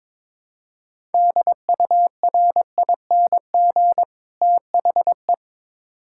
Morse code, heard as a keyed tone, about 23 wpm, 700 Hz. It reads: DURING THE
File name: Build A Phrase - 2-Word Phrases - 22wpm